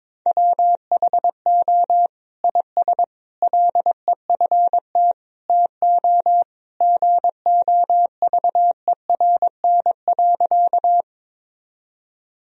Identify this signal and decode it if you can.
Morse code, heard as a keyed tone, about 22 wpm, 700 Hz. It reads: WHO IS LEFT TO GOVERN.
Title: Sentences from Top 1000 Words - Review Part 2 - 22wpm